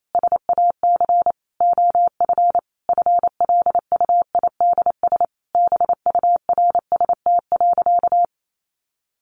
Morse code, heard as a keyed tone, about 28 wpm, 700 Hz. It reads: SAC OF FLUIDS BURST.
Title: A Sound Of Thunder 28wpm